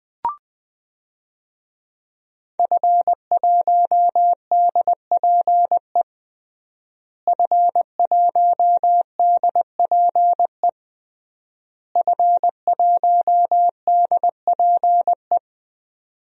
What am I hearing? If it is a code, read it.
Morse code, heard as a keyed tone, about 20 wpm, 700 Hz. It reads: F1DPE F1DPE F1DPE
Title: DXCC-100 Call Signs - 5 Characters - Repeated 3x - 20wpm